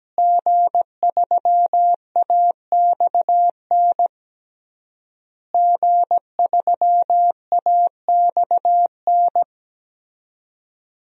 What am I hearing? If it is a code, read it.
Morse code, heard as a keyed tone, about 17 wpm, 700 Hz. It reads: G3AXN G3AXN
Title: DXCC-100 Call Signs - 5 Characters - Repeated 3x - 17wpm